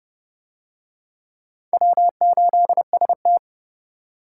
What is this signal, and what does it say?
Morse code, heard as a keyed tone, about 28 wpm, 700 Hz. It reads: W8ST